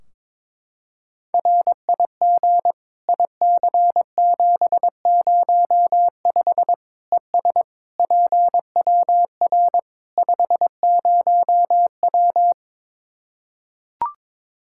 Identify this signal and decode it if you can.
Morse code, heard as a keyed tone, about 23 wpm, 700 Hz. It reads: RIG IC705 ES PWR 50W
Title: Practice QSO - Exchange 2 - Info - 22wpm